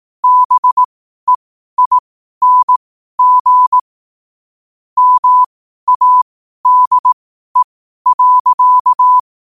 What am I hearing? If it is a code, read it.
Morse code, heard as a keyed tone, about 18 wpm, spaced as for 12 wpm, 1000 Hz. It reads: BEING MADE.
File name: qotdfiles / 12wpm